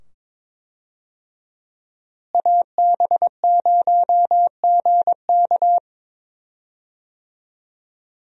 Morse code, heard as a keyed tone, about 22 wpm, 700 Hz. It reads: AB0GK
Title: USA Call Signs - 5 Characters - Repeated 2x - 22wpm